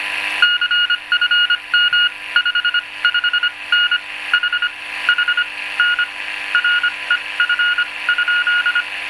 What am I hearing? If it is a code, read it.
Morse code, heard as a keyed tone, about 25 wpm, spaced as for 21 wpm, 1440 Hz, with noise. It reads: CFM 55N H H N REF?